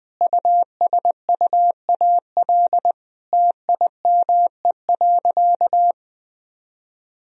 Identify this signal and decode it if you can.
Morse code, heard as a keyed tone, about 20 wpm, 700 Hz. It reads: USUAL TIME.
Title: Sentences from Top 1000 Words - Review Part 2 - 20wpm